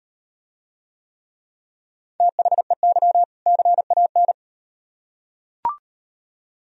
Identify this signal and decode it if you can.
Morse code, heard as a keyed tone, about 36 wpm, 700 Hz. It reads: THEY CAN